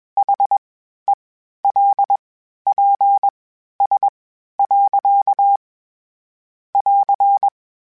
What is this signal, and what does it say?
Morse code, heard as a keyed tone, about 21 wpm, spaced as for 12 wpm, 800 Hz. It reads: HELPS. <AR>